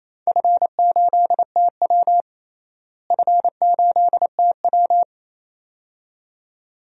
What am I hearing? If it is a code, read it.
Morse code, heard as a keyed tone, about 28 wpm, 700 Hz. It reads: F8TW F8TW
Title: DXCC-100 Call Signs - 4 Characters - Repeated 2x - 28wpm